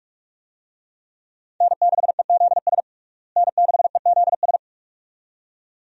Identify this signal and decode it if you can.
Morse code, heard as a keyed tone, about 46 wpm, 700 Hz. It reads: N6EZS N6EZS
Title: USA Call Signs - 5 Characters - Repeated 2x - 45wpm